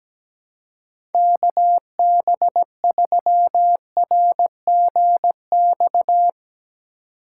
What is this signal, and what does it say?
Morse code, heard as a keyed tone, about 17 wpm, 700 Hz. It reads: KB3RGX
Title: USA Call Signs - 6 Characters - 3x Character Spacing - 17wpm